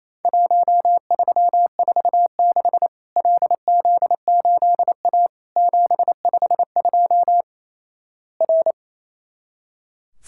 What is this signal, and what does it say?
Morse code, heard as a keyed tone, about 28 wpm, 700 Hz. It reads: 1346 LZ8A 752 R
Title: Worked All Europe DX Contest - QTC Exchange - 28wpm